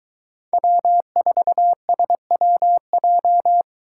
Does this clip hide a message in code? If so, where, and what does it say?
Yes: Morse code from 0.5 to 3.6 s, heard as a keyed tone, about 23 wpm, 700 Hz. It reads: W4SWJ